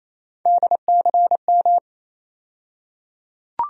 Morse code, heard as a keyed tone, about 27 wpm, 700 Hz. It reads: DCM
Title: USA Call Sign Postfixes - 3 Letters 28wpm